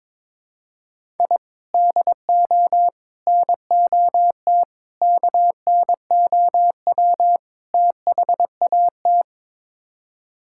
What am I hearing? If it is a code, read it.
Morse code, heard as a keyed tone, about 22 wpm, 700 Hz. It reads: I DO NOT KNOW THAT
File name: practice / 22wpm / Build A Phrase - 5-Word Phrases - 22wpm